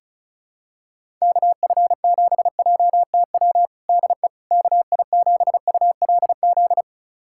Morse code, heard as a keyed tone, about 36 wpm, 700 Hz. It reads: KF7JTW DE KI7ULZ
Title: USA Callsign DE Callsign 35wpm